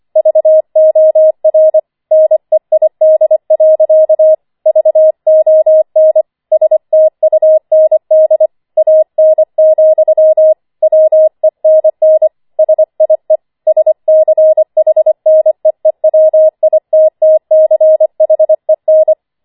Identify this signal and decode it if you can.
Morse code, heard as a keyed tone, about 25 wpm, 630 Hz. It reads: VOR NEID. VON STUND AN, WENN SIE SCHNEEWITTCHEN
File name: 25wpm